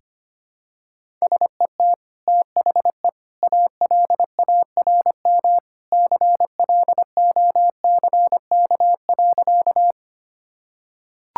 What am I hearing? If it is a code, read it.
Morse code, heard as a keyed tone, about 25 wpm, 700 Hz. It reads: SET THE ALARM CLOCK.
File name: Easy Sentences with Hints - 25wpm